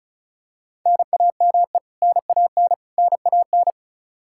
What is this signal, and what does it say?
Morse code, heard as a keyed tone, about 36 wpm, 700 Hz. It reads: NAME NAN NAN
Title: Practice QSO - Exchange 1 - Beginning - 35wpm